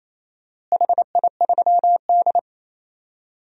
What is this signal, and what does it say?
Morse code, heard as a keyed tone, about 27 wpm, 700 Hz. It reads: HI3D
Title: DXCC-100 Call Signs - 4 Characters - Repeated 3x - 28wpm